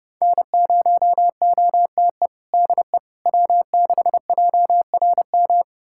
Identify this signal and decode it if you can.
Morse code, heard as a keyed tone, about 30 wpm, 700 Hz. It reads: N0OTE DE W6JRM
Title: USA Callsign DE Callsign - 5 Characters - 30wpm